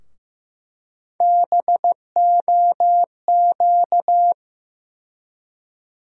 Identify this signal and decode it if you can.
Morse code, heard as a keyed tone, about 15 wpm, 700 Hz. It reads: BOQ
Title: USA Call Sign Postfixes - 3 Letters 15wpm